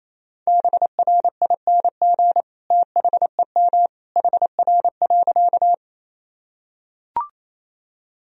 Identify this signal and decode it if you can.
Morse code, heard as a keyed tone, about 28 wpm, 700 Hz. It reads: BRING THEM HR.
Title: Sentences from Top 500 Words and Abbreviations 28wpm